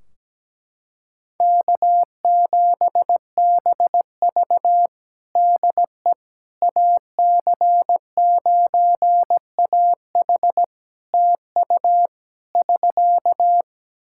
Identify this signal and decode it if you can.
Morse code, heard as a keyed tone, about 17 wpm, 700 Hz. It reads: K7BV DE AC9AH TU <SK>
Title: Practice QSO - Exchange 4 - Ending - 17wpm